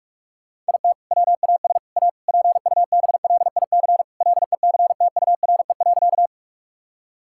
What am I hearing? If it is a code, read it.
Morse code, heard as a keyed tone, about 47 wpm, 700 Hz. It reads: IT WAS A PUBLIC LECTURE.
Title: Sentences from Top 1600 Words 45wpm